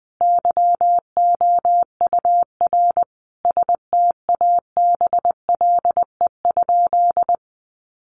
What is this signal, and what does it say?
Morse code, heard as a keyed tone, about 20 wpm, 700 Hz. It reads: YOUR STABLE?
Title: Sentences from Top 1500 Words 20wpm